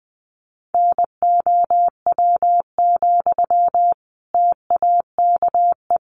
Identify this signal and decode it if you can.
Morse code, heard as a keyed tone, about 20 wpm, 700 Hz. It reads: NOW, TAKE
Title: Sentences from Top 400 Words 20wpm